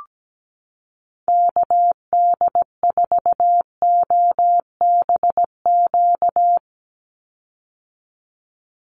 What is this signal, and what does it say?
Morse code, heard as a keyed tone, about 17 wpm, 700 Hz. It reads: KD4OBQ
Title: USA Call Signs 17wpm